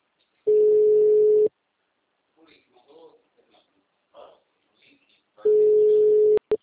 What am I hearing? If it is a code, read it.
Ringback tone (Europe, 425 Hz), with speech.